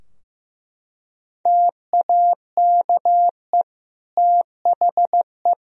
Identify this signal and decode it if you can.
Morse code, heard as a keyed tone, about 15 wpm, 700 Hz. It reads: TAKE THE
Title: Sentences from Top 1000 Words - Review Part 2 - 15wpm